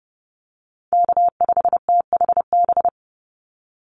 Morse code, heard as a keyed tone, about 30 wpm, 700 Hz. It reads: K5THB
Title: USA Call Signs 30wpm